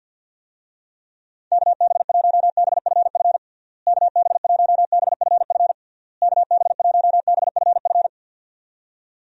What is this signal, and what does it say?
Morse code, heard as a keyed tone, about 50 wpm, 700 Hz. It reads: KB1BFF KB1BFF KB1BFF
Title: USA Call Signs - 6 Characters - Repeated 3x - 50wpm